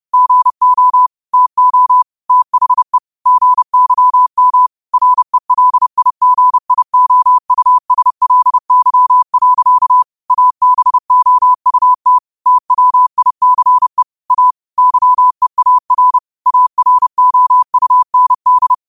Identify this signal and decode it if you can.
Morse code, heard as a keyed tone, about 30 wpm, 1000 Hz. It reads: GO TO THE GYM RELIGIOUSLY. ABOUT TWICE A YEAR AROUND
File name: qotdfiles / 30wpm